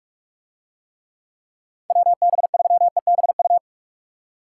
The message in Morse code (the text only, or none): WB3EBU